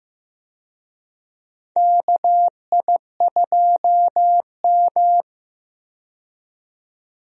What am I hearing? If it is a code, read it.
Morse code, heard as a keyed tone, about 15 wpm, 700 Hz. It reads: KI2M
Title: USA Call Signs - 4 Characters - 15wpm